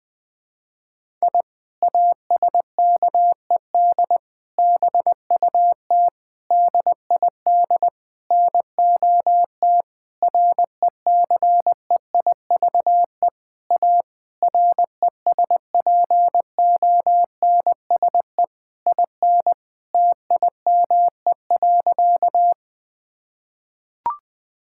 Morse code, heard as a keyed tone, about 21 wpm, 700 Hz. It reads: I ASKED BUT DID NOT RECEIVE A RESPONSE IN TIME.
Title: Sentences with 8-Word Phrases - 20wpm